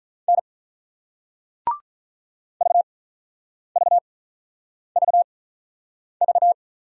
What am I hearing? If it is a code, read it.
Morse code, heard as a keyed tone, about 48 wpm, spaced as for 9 wpm, 700 Hz. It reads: N VVVV